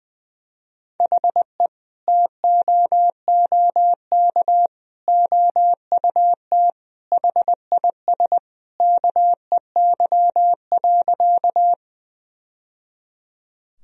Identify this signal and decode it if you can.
Morse code, heard as a keyed tone, about 21 wpm, 700 Hz. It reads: HE TOOK OUT HIS KEY.
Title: Sentences from Top 1000 Words - Review Part 4 - 20wpm